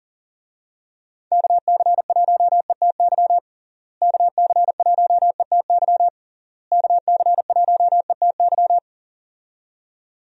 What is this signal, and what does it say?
Morse code, heard as a keyed tone, about 40 wpm, 700 Hz. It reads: KC1ETY KC1ETY KC1ETY
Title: USA Call Signs - 6 Characters - Repeated 3x - 40wpm